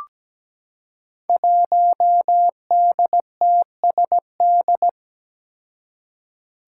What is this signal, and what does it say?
Morse code, heard as a keyed tone, about 17 wpm, 700 Hz. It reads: 1DTSD
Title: Five Challenging Letter-Number Combinations 17wpm